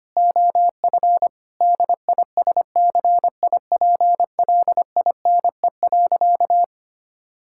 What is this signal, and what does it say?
Morse code, heard as a keyed tone, about 25 wpm, 700 Hz. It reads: OF DISCIPLINE.
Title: Sentences from Top 1000 Words - Review Part 6 - 25wpm